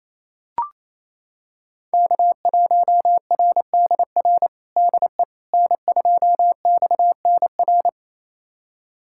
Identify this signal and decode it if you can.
Morse code, heard as a keyed tone, about 28 wpm, 700 Hz. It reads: K1RDR DE N2XNR
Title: USA Callsign DE Callsign - 5 Characters - 28wpm